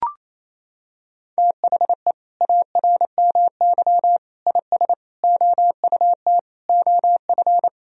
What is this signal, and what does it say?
Morse code, heard as a keyed tone, about 28 wpm, 700 Hz. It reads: THE ARMY IS OUT OF
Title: Sentences from Top 1000 Words - Review Part 4 - 28wpm